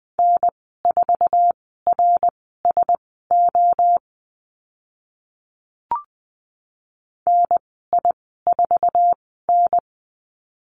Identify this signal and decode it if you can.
Morse code, heard as a keyed tone, about 20 wpm, spaced as for 14 wpm, 700 Hz. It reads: N4RSO NI4N
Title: 082 Lesson 14g - Number 2 - Call Signs 20wpm